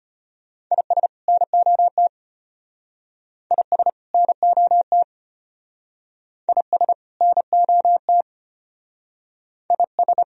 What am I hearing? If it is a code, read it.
Morse code, heard as a keyed tone, about 33 wpm, 700 Hz. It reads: IS NOT IS NOT IS NOT IS